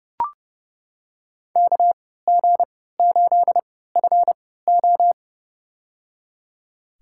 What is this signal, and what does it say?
Morse code, heard as a keyed tone, about 30 wpm, spaced as for 17 wpm, 700 Hz. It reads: KG8FO
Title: USA Call Signs - 5 Characters - 3x Character Spacing - 30wpm